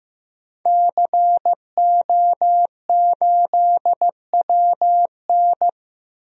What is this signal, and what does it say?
Morse code, heard as a keyed tone, about 15 wpm, 700 Hz. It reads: CO8WN
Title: DXCC-100 Call Signs - 5 Characters - Repeated 2x - 15wpm